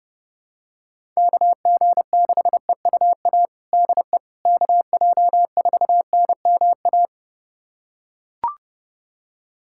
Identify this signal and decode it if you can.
Morse code, heard as a keyed tone, about 30 wpm, 700 Hz. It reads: KG6EUA DE KJ4NMA
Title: USA Callsign DE Callsign - 6 Characters - 30wpm